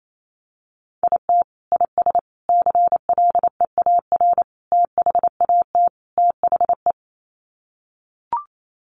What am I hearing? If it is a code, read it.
Morse code, heard as a keyed tone, about 28 wpm, 700 Hz. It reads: IT IS CLEAR THAT THE
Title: Sets of 5 Words Encore 28wpm